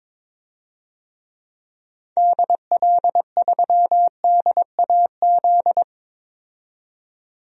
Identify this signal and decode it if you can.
Morse code, heard as a keyed tone, about 22 wpm, 700 Hz. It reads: DL3DAZ
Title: DXCC-100 Call Signs - 6 Characters - 22wpm